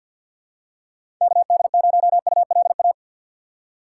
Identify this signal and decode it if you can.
Morse code, heard as a keyed tone, about 50 wpm, 700 Hz. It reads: KD0ULA
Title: USA Call Signs - 6 Characters - 4x Character Spacing - 50wpm